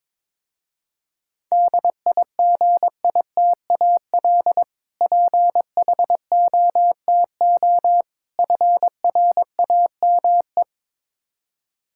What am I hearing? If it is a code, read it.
Morse code, heard as a keyed tone, about 21 wpm, 700 Hz. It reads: DIGITAL PHOTO FRAME
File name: Difficult Words with Hints - 22wpm